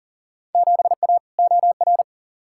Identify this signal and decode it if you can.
Morse code, heard as a keyed tone, about 40 wpm, 700 Hz. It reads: 7A OR
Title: Field Day Exchanges Encore - 40wpm